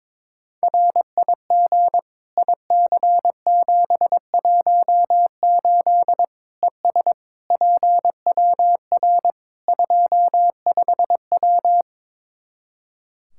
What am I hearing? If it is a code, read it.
Morse code, heard as a keyed tone, about 21 wpm, 700 Hz. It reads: RIG IC718 ES PWR 25W
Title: Practice QSO - Exchange 2 - Info - 22wpm